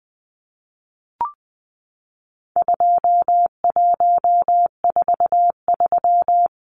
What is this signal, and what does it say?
Morse code, heard as a keyed tone, about 20 wpm, 700 Hz. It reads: 2143
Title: Numbers - 4-Digit 20wpm